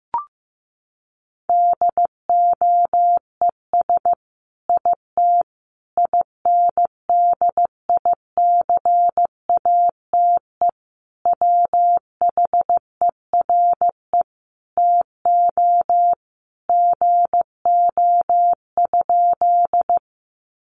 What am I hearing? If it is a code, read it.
Morse code, heard as a keyed tone, about 15 wpm, 700 Hz. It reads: DOES IT INDICATE WHERE TO GO?
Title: Sentences from Top 1300 Words 15wpm